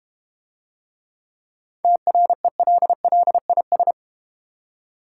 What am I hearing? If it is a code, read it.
Morse code, heard as a keyed tone, about 32 wpm, 700 Hz. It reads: TRELLIS